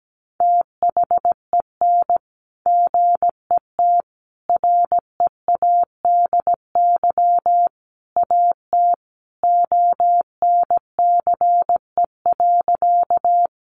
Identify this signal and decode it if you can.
Morse code, heard as a keyed tone, about 17 wpm, 700 Hz. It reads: THEN GET READY AT ONCE.
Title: Sentences from Top 400 Words 17wpm